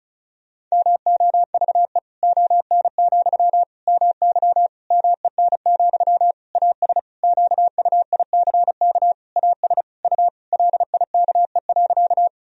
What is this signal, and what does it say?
Morse code, heard as a keyed tone, about 36 wpm, 700 Hz. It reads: MOVE ON, MY MEN, AS QUICK AS U LIKE.
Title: Sentences from Top 500 Words and Abbreviations 35wpm